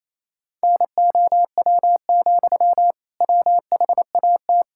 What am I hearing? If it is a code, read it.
Morse code, heard as a keyed tone, about 28 wpm, 700 Hz. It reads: NOW, WHAT
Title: Sentences from Top 1600 Words 28wpm